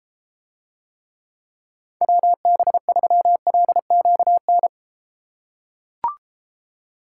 Morse code, heard as a keyed tone, about 31 wpm, 700 Hz. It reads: WB3LQN